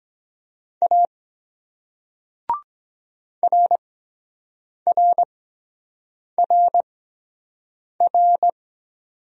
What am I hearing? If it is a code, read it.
Morse code, heard as a keyed tone, about 23 wpm, spaced as for 7 wpm, 700 Hz. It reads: A RRRR